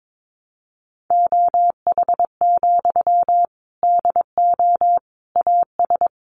Morse code, heard as a keyed tone, about 22 wpm, 700 Hz. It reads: OH, DO AS
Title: Sentences from Top 500 Words 22wpm